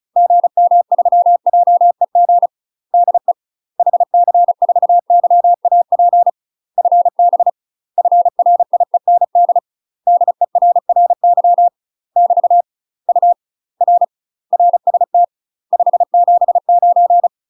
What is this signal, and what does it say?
Morse code, heard as a keyed tone, about 36 wpm, 700 Hz. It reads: GM3JEG DE HC4YAP FB FRIEND DERRY <BT> U R RST 579
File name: UK MASTER QSOs TRAINING / 35WPM / UK MASTER QSOs 0013